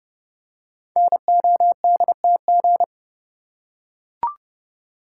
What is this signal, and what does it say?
Morse code, heard as a keyed tone, about 30 wpm, 700 Hz. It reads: NODTG
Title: Code Groups - 5 Letters - 30wpm